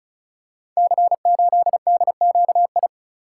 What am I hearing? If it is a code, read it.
Morse code, heard as a keyed tone, about 36 wpm, 700 Hz. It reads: C8DQI
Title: Code Groups - 5 Letter:Numbers - 35wpm